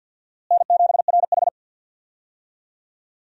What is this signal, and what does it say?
Morse code, heard as a keyed tone, about 52 wpm, 700 Hz. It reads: N7RH
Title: USA Call Signs - 4 Characters - Repeated 3x - 50wpm